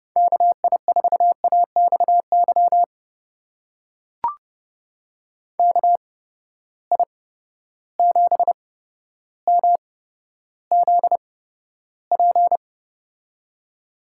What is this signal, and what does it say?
Morse code, heard as a keyed tone, about 30 wpm, spaced as for 13 wpm, 700 Hz. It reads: KI4AXY KI7MZP
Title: USA Call Signs - 8x Character Spacing - 30wpm